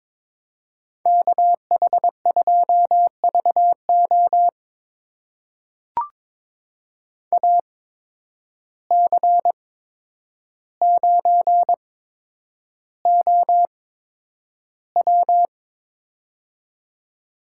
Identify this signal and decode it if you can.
Morse code, heard as a keyed tone, about 22 wpm, spaced as for 9 wpm, 700 Hz. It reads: KH2VO AC9OW
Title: USA Call Signs - 5 Characters - 8x Character Spacing - 22wpm